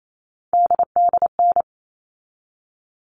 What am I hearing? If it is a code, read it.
Morse code, heard as a keyed tone, about 28 wpm, 700 Hz. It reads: DDN
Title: All Cut Numbers - 3-Digit 28wpm